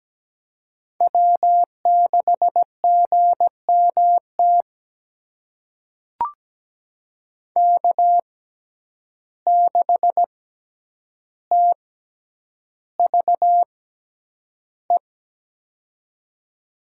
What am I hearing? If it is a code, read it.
Morse code, heard as a keyed tone, about 17 wpm, spaced as for 9 wpm, 700 Hz. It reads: W6GMT K6TVE
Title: USA Call Signs - 5 Characters - 6x Character Spacing - 17wpm